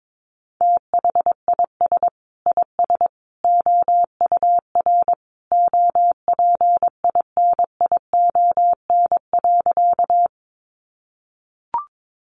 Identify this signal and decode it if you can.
Morse code, heard as a keyed tone, about 22 wpm, 700 Hz. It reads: THIS IS OUR OPINION.
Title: Sentences from Top 1200 Words 22wpm